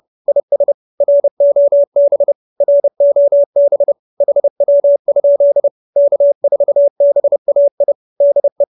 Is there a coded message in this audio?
Yes: Morse code from 0.3 to 8.6 s, heard as a keyed tone, about 30 wpm, 590 Hz. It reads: IS ROB ROB HW? K4BAI DE